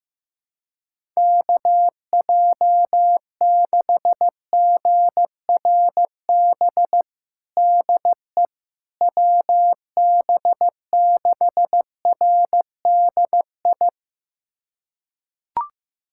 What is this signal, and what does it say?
Morse code, heard as a keyed tone, about 16 wpm, 700 Hz. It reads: KJ6GRB DE WB6RDI
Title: USA Callsign DE Callsign - 6 Characters - 15wpm